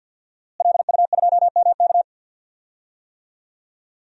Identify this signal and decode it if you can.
Morse code, heard as a keyed tone, about 45 wpm, 700 Hz. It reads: LU2MK
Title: DXCC-100 Call Signs - 5 Characters - Repeated 3x - 50wpm